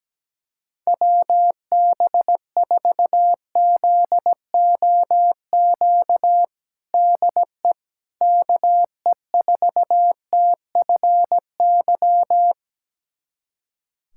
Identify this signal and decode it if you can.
Morse code, heard as a keyed tone, about 17 wpm, 700 Hz. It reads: WB4ZOQ DE KE4TFY
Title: USA Callsign DE Callsign 17wpm